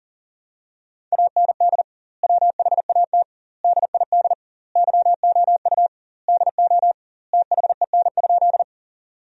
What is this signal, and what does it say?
Morse code, heard as a keyed tone, about 40 wpm, 700 Hz. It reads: AND WHAT DID YOU DO THEN?
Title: 2x Word-Spacing - Sentences from Top 100 Words - 40wpm